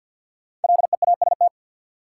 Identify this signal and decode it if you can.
Morse code, heard as a keyed tone, about 52 wpm, 700 Hz. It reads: LEAST
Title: Sets of 2 Words - New Word Drill - 50wpm